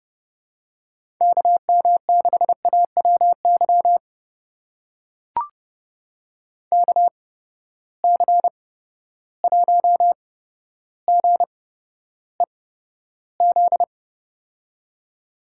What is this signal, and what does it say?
Morse code, heard as a keyed tone, about 30 wpm, spaced as for 13 wpm, 700 Hz. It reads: KM6AWY KC1GEZ
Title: USA Call Signs - 6 Characters - 8x Character Spacing - 30wpm